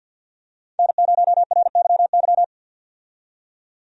Morse code, heard as a keyed tone, about 53 wpm, 700 Hz. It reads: N0RYY
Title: USA Call Signs - 5 Characters - 50wpm